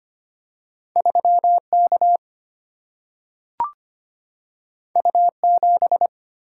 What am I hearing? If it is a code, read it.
Morse code, heard as a keyed tone, about 25 wpm, 700 Hz. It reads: 3 K U 7
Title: Single-Letter Single-Number 25wpm